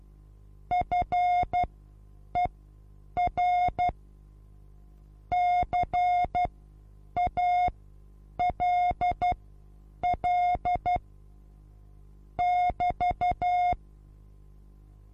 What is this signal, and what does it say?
Morse code, heard as a keyed tone, about 12 wpm, spaced as for 8 wpm, 720 Hz. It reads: FER CALL <BT>
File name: Onlinelehrgang / morsen / DJ4UF Morsen23